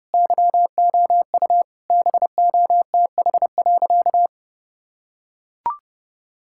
Morse code, heard as a keyed tone, about 32 wpm, 700 Hz. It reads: YOU BOTH.
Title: Sentences from Top 1000 Words - Review Part 6 - 30wpm